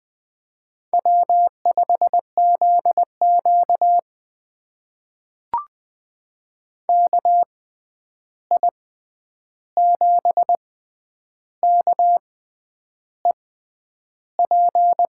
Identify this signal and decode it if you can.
Morse code, heard as a keyed tone, about 20 wpm, spaced as for 9 wpm, 700 Hz. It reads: W5ZQ KI7KEP
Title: USA Call Signs - 6x Character Spacing - 20wpm